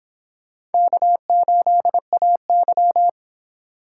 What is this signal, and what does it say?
Morse code, heard as a keyed tone, about 31 wpm, 700 Hz. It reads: K8AY